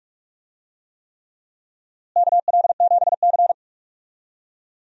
Morse code, heard as a keyed tone, about 46 wpm, 700 Hz. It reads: KL7C
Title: DXCC-100 Call Signs - 4 Characters - Repeated 3x - 45wpm